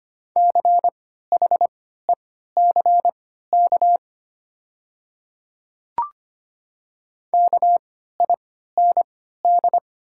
Morse code, heard as a keyed tone, about 25 wpm, spaced as for 14 wpm, 700 Hz. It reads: CHECK KIND
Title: 3x Character Spacing - Top 500 Words - 25wpm